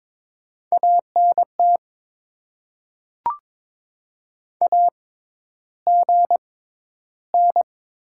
Morse code, heard as a keyed tone, about 22 wpm, spaced as for 11 wpm, 700 Hz. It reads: ANT AGN
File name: Top 20 QSO Elements - 6x Character Spacing - 22wpm